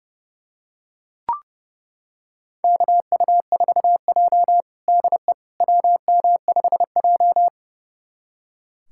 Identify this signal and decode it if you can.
Morse code, heard as a keyed tone, about 31 wpm, 700 Hz. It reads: KU4J DE WM5J
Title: USA Callsign DE Callsign - 4 Characters - 30wpm